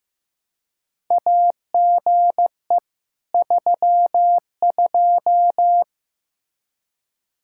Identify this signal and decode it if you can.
Morse code, heard as a keyed tone, about 15 wpm, 700 Hz. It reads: AGE 32
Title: Practice QSO - Exchange 3 - More Info - 15wpm